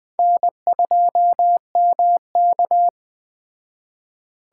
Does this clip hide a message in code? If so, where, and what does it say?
Yes: Morse code from 0.2 to 2.9 s, heard as a keyed tone, about 20 wpm, 700 Hz. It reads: N2MK